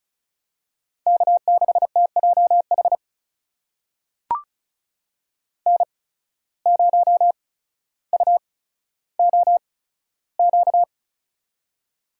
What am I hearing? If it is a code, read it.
Morse code, heard as a keyed tone, about 36 wpm, spaced as for 15 wpm, 700 Hz. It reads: K6TJH N0UOQ
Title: USA Call Signs - 5 Characters - 8x Character Spacing - 35wpm